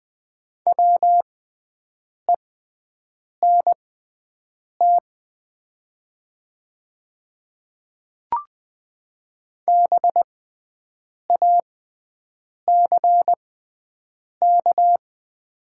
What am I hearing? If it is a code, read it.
Morse code, heard as a keyed tone, about 20 wpm, spaced as for 7 wpm, 700 Hz. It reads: WENT BACK
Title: 6x Character Spacing - Top 300 Words - 20wpm